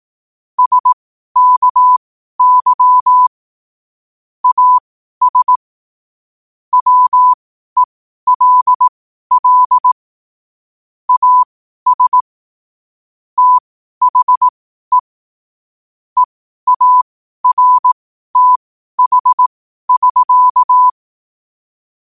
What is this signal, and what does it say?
Morse code, heard as a keyed tone, about 18 wpm, spaced as for 12 wpm, 1000 Hz. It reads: SKY AS WELL AS THE EARTH<SK>
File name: qotdfiles / 12wpm